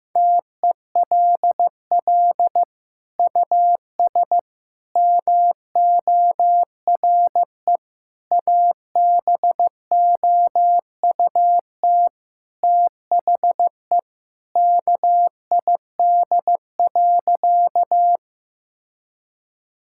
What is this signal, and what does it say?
Morse code, heard as a keyed tone, about 15 wpm, 700 Hz. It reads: TELL US MORE ABOUT THE KID.
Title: Sentences from Top 1600 Words 15wpm